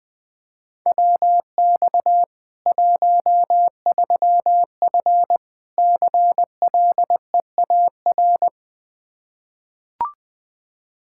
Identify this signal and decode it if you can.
Morse code, heard as a keyed tone, about 21 wpm, 700 Hz. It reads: WX 13F CLEAR
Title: K1USN SST QSO Practice 20wpm